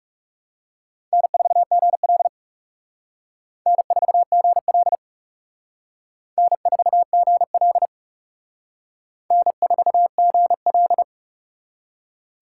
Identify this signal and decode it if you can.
Morse code, heard as a keyed tone, about 39 wpm, 700 Hz. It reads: N4GL N4GL N4GL N4GL